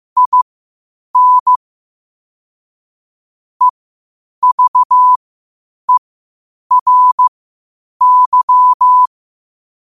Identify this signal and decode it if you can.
Morse code, heard as a keyed tone, about 15 wpm, spaced as for 8 wpm, 1000 Hz. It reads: IN EVERY
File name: qotdfiles / 8wpm